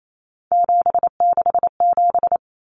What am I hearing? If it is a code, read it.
Morse code, heard as a keyed tone, about 28 wpm, 700 Hz. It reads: 767
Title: Numbers - 3-Digit 28wpm